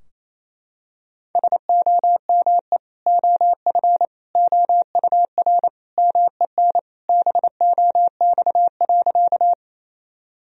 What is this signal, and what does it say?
Morse code, heard as a keyed tone, about 28 wpm, 700 Hz. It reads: SOME OF OUR MEN BOX.
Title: Sentences from Top 1000 Words - Review Part 2 - 28wpm